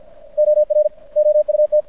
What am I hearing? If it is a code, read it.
Morse code, heard as a keyed tone, about 53 wpm, 600 Hz. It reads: OR OUT
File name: archived / CRSnet